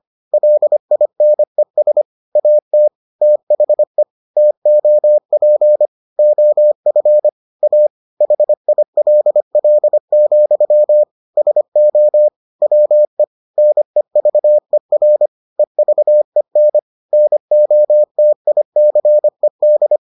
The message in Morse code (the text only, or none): LINES AT THE TOP OF A HILL, SO WE NEVER EVEN NOTICED